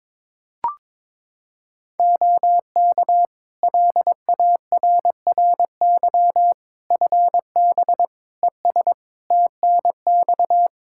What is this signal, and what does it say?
Morse code, heard as a keyed tone, about 21 wpm, 700 Hz. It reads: OK LARRY FB ES TNX
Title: Practice QSO - Exchange 2 - Info - 22wpm